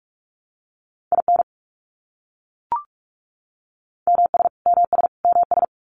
Morse code, heard as a keyed tone, about 44 wpm, 700 Hz. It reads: IN MS MS MS
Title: US States Repeated 3x 45wpm